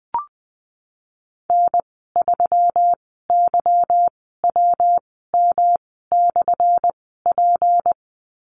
Morse code, heard as a keyed tone, about 20 wpm, spaced as for 15 wpm, 700 Hz. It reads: N3YWM/P
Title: 179 Lesson 29g - Slash - Call Signs 20wpm